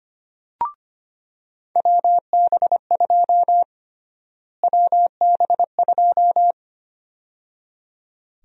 Morse code, heard as a keyed tone, about 25 wpm, 700 Hz. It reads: WB2 WB2
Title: USA Call Sign Prefixes - Repeated 2x - 25wpm